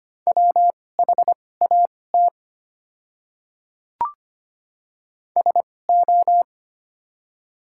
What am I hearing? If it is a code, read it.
Morse code, heard as a keyed tone, about 25 wpm, spaced as for 18 wpm, 700 Hz. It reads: WHAT SO
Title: 2x Character Spacing - Top 300 Words - 25wpm